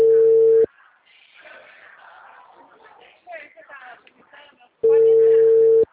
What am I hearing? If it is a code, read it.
Ringback tone (China, 450 Hz), with speech.
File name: out- -2013- - - .761